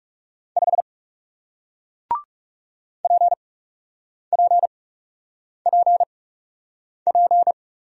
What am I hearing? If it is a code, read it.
Morse code, heard as a keyed tone, about 38 wpm, spaced as for 8 wpm, 700 Hz. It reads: 5 PPPP